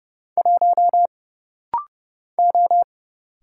Morse code, heard as a keyed tone, about 30 wpm, 700 Hz. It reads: 1 O
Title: Single Letter-Number - Rapid-Fire 30wpm